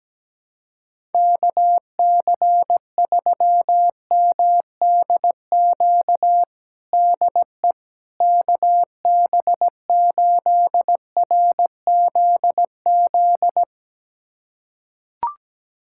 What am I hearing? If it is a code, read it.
Morse code, heard as a keyed tone, about 17 wpm, 700 Hz. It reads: KC3MDQ DE KB8RZZ
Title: USA Callsign DE Callsign 17wpm